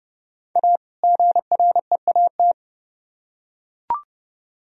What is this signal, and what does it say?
Morse code, heard as a keyed tone, about 30 wpm, 700 Hz. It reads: A GREAT